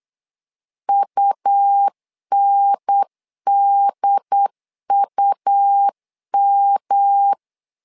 Morse code, heard as a keyed tone, about 8 wpm, 800 Hz. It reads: UNDUM
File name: LEZIONI / 3 MWRO / 3 M 8wpm 2300